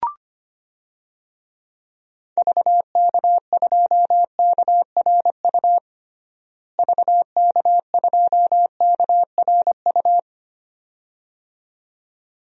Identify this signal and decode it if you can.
Morse code, heard as a keyed tone, about 25 wpm, 700 Hz. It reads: VK2KRU VK2KRU
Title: DXCC-100 Call Signs - 6 Characters - Repeated 2x - 25wpm